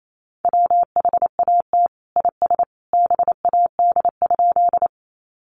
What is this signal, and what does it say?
Morse code, heard as a keyed tone, about 28 wpm, 700 Hz. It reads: WHAT IS BAD?
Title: Sentences from Top 700 Words 28wpm